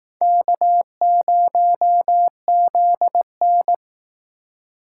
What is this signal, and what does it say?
Morse code, heard as a keyed tone, about 19 wpm, 700 Hz. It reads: K0ZN